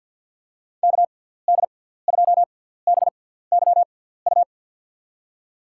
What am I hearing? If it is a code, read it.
Morse code, heard as a keyed tone, about 53 wpm, spaced as for 17 wpm, 700 Hz. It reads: KD2BYU
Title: USA Call Signs - 6 Characters - 6x Character Spacing - 50wpm